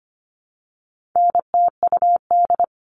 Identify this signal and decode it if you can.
Morse code, heard as a keyed tone, about 25 wpm, 700 Hz. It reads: NTUD
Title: All Cut Numbers - 4-Digit 25wpm